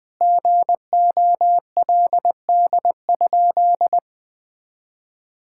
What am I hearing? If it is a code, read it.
Morse code, heard as a keyed tone, about 21 wpm, 700 Hz. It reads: GOLD?
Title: Sentences from Top 1600 Words 20wpm